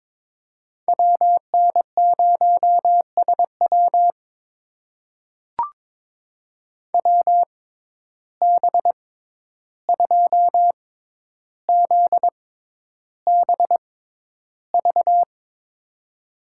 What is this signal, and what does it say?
Morse code, heard as a keyed tone, about 21 wpm, spaced as for 11 wpm, 700 Hz. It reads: WN0SW WB2ZBV
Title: USA Call Signs - 6x Character Spacing - 22wpm